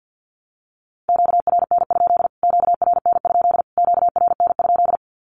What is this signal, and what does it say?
Morse code, heard as a keyed tone, about 50 wpm, 700 Hz. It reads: QRN? QRN? QRN?
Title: Q-Codes Repeated 3x 50wpm